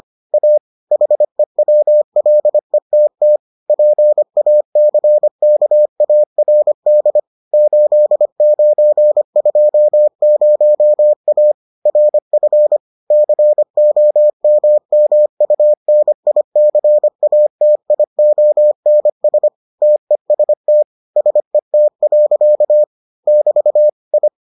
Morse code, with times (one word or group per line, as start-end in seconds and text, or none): 0.3-0.6 A
0.9-3.4 HEWLETT
3.7-7.2 PACKARD
7.5-11.5 8920A
11.9-12.8 RF
13.1-19.5 COMMUNICATIONS
19.8-20.8 TEST
21.2-22.8 SET.
23.3-23.9 <BT>
24.1-24.3 I